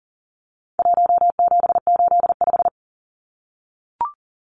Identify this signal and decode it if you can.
Morse code, heard as a keyed tone, about 40 wpm, 700 Hz. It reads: 1785
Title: Numbers - 4-Digit 40wpm